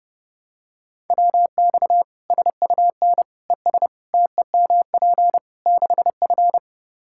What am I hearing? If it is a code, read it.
Morse code, heard as a keyed tone, about 30 wpm, 700 Hz. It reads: WX SUN ES TEMP 6F
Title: Practice QSO - Exchange 2 - Info - 30wpm